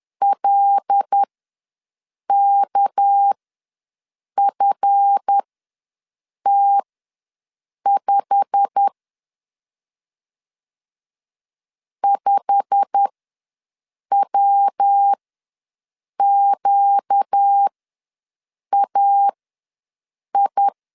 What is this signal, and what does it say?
Morse code, heard as a keyed tone, about 10 wpm, spaced as for 6 wpm, 800 Hz. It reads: LKFT5 5WQAI
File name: LEZIONI / 8 YJ45 / 8 YJ45 ws 10wpm 3843